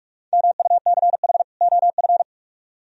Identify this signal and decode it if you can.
Morse code, heard as a keyed tone, about 44 wpm, 700 Hz. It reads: MUCH OF